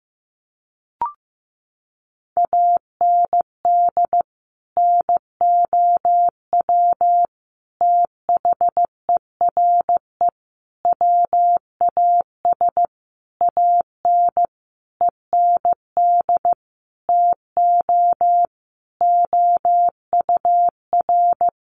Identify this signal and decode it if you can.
Morse code, heard as a keyed tone, about 15 wpm, 700 Hz. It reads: AND NOW THERE WAS AN END TO OUR
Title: Sentences from Top 1400 Words 15wpm